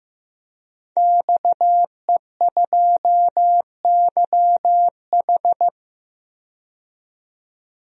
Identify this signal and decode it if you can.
Morse code, heard as a keyed tone, about 16 wpm, 700 Hz. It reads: XE2YH
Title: DXCC-100 Call Signs - 5 Characters - 15wpm